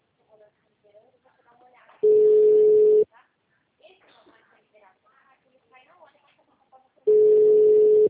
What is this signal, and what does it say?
Ringback tone (Europe, 425 Hz), with speech.